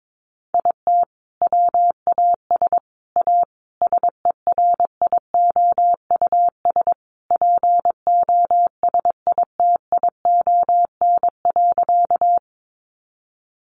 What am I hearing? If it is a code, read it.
Morse code, heard as a keyed tone, about 22 wpm, 700 Hz. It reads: IT WAS A SERIOUS POSITION.
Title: Sentences from Top 1100 Words 22wpm